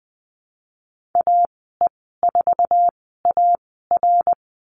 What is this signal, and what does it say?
Morse code, heard as a keyed tone, about 20 wpm, spaced as for 15 wpm, 700 Hz. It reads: AE4AR
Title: 076 Lesson 13g - Letter L - Call Signs 20wpm